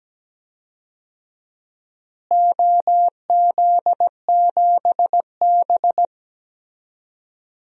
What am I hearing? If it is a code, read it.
Morse code, heard as a keyed tone, about 17 wpm, 700 Hz. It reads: OZ7B
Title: DXCC-100 Call Signs - 4 Characters - Repeated 3x - 17wpm